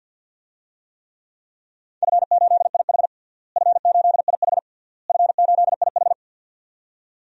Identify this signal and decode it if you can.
Morse code, heard as a keyed tone, about 52 wpm, 700 Hz. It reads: F8IH F8IH F8IH
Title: DXCC-100 Call Signs - 4 Characters - Repeated 3x - 50wpm